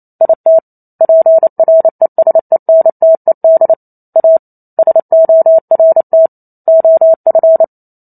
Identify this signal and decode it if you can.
Morse code, heard as a keyed tone, about 28 wpm, 650 Hz. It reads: IT PRESENTED A SORT OF